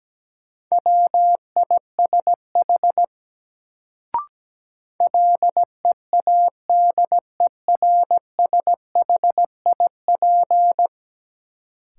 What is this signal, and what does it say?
Morse code, heard as a keyed tone, about 17 wpm, 700 Hz. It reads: WISH LEADERSHIP
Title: Sentences from Top 1600 Words - New Word Drill Alt - 17wpm